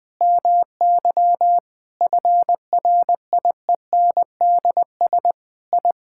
Morse code, heard as a keyed tone, about 20 wpm, 700 Hz. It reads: MY FRIENDS I
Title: Sentences with 7-Word Phrases - Spoken Beginning - 20wpm